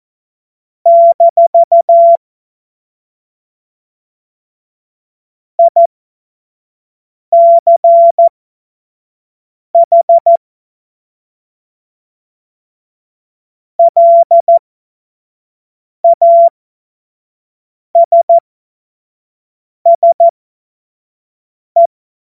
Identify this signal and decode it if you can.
Morse code, heard as a keyed tone, about 14 wpm, spaced as for 5 wpm, 680 Hz. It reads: - ICH LASSE
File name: Morsekurs / Witze / Witz 22. Februar 2022 25 F70